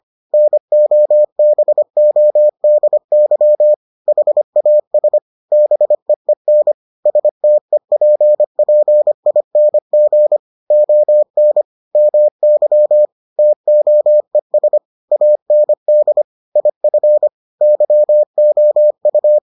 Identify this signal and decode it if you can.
Morse code, heard as a keyed tone, about 25 wpm, 600 Hz. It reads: NOBODY HAS BEEN STEPPING ON MY TOES AND IF YOU